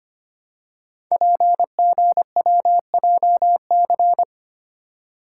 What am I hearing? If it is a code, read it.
Morse code, heard as a keyed tone, about 25 wpm, 700 Hz. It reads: PGWJC
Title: Code Groups - 5 Letters - 25wpm